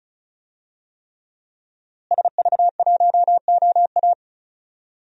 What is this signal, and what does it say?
Morse code, heard as a keyed tone, about 36 wpm, 700 Hz. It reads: SV1OA
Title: DXCC-100 Call Signs - 5 Characters - Repeated 3x - 35wpm